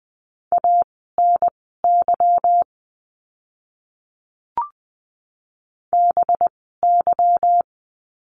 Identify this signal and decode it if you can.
Morse code, heard as a keyed tone, about 20 wpm, spaced as for 14 wpm, 700 Hz. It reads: ANY BY
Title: 192 Lesson 31f - Letter v - Words 20wpm